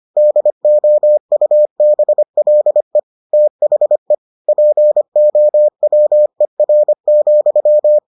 Morse code, heard as a keyed tone, about 25 wpm, 600 Hz. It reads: DOUBLE THE POWER,